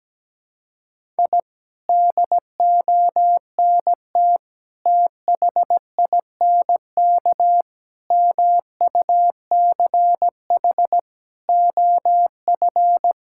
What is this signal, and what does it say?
Morse code, heard as a keyed tone, about 17 wpm, 700 Hz. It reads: I DONT THINK MUCH OF
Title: Sentences from Top 1000 Words - Review Part 6 - 17wpm